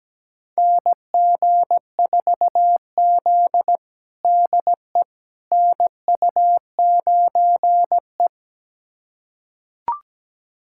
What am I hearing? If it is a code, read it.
Morse code, heard as a keyed tone, about 17 wpm, 700 Hz. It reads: NG4Z DE NU9E
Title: USA Callsign DE Callsign - 4 Characters - 17wpm